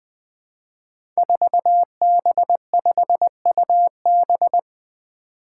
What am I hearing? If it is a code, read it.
Morse code, heard as a keyed tone, about 21 wpm, 700 Hz. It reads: 4B5UB
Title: Five Challenging Letter-Number Combinations 20wpm